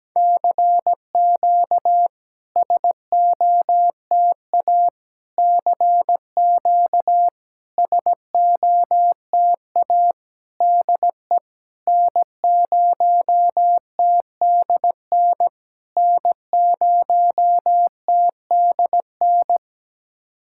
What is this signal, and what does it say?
Morse code, heard as a keyed tone, about 17 wpm, 700 Hz. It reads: CQ SOTA CQ SOTA DE N0TDN N0TDN
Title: SOTA QSO Practice 17wpm